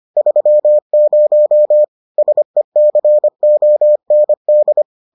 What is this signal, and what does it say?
Morse code, heard as a keyed tone, about 25 wpm, 600 Hz. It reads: 30 SECOND